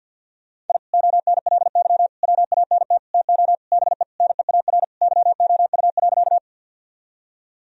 Morse code, heard as a keyed tone, about 52 wpm, 700 Hz. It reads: I ONLY WANT TO BE NEAR YOU.
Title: Sentences from Top 1000 Words - Review Part 1 - 50wpm